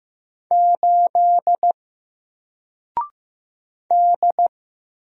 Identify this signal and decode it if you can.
Morse code, heard as a keyed tone, about 15 wpm, 700 Hz. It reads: 8 D
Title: Single Letter-Number 15wpm